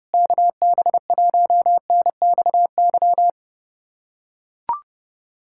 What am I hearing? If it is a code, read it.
Morse code, heard as a keyed tone, about 30 wpm, 700 Hz. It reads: KB1NXY
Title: USA Callsign DE Callsign - 6 Characters - 30wpm